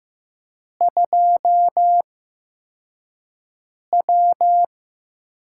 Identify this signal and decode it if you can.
Morse code, heard as a keyed tone, about 15 wpm, spaced as for 4 wpm, 700 Hz. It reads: 2W
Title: USA Call Signs - 4 Characters - 8x Character Spacing - 15wpm